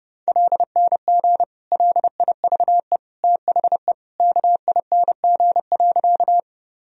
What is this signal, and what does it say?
Morse code, heard as a keyed tone, about 32 wpm, 700 Hz. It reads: LNG LIVE THE KING.
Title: Sentences from Top 400 Words and Abbreviations 30wpm